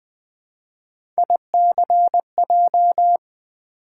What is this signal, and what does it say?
Morse code, heard as a keyed tone, about 21 wpm, 700 Hz. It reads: ICJ
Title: Three Letter-Number Combinations - 20wpm